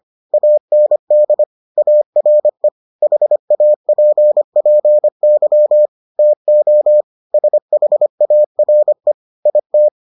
Morse code, heard as a keyed tone, about 25 wpm, 600 Hz. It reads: AND ARE HAPPY TO SHARE IT